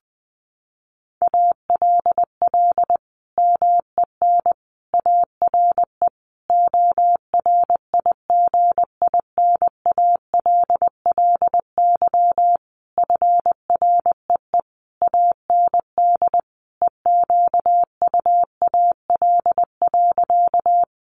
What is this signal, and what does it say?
Morse code, heard as a keyed tone, about 20 wpm, 700 Hz. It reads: ALL MEN ARE ORIGINALLY FREE AND EQUAL.
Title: Sentences from Top 1500 Words 20wpm